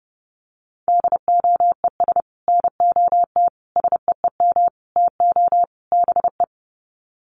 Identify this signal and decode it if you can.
Morse code, heard as a keyed tone, about 30 wpm, 700 Hz. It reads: DOES NOT SEEM TO BE
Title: Sets of 5 Words 30wpm